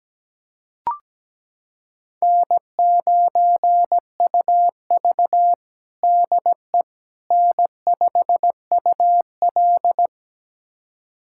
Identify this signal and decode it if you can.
Morse code, heard as a keyed tone, about 17 wpm, 700 Hz. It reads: N9UV DE N5UL
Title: USA Callsign DE Callsign - 4 Characters - 17wpm